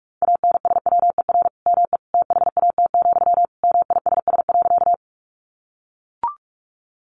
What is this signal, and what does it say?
Morse code, heard as a keyed tone, about 44 wpm, 700 Hz. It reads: ANSWER ME THAT, MISS.
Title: Sentences from Top 500 Words 45wpm